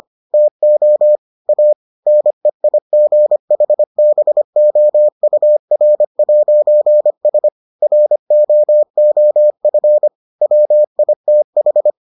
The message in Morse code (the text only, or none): TO A NEIGHBOUR'S ROOF WITH